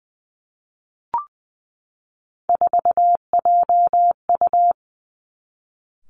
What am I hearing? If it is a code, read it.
Morse code, heard as a keyed tone, about 20 wpm, 700 Hz. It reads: 4JU
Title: Three Challenging Letter-Number Combinations 20wpm